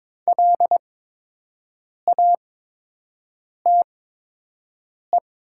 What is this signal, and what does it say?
Morse code, heard as a keyed tone, about 22 wpm, spaced as for 6 wpm, 700 Hz. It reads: LATE
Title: 8x Character Spacing - Top 300 Words - 22wpm